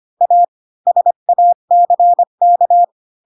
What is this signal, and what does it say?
Morse code, heard as a keyed tone, about 25 wpm, 700 Hz. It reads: A SACK
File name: Pinocchio / Pinocchio ch7 20 25 ik1hge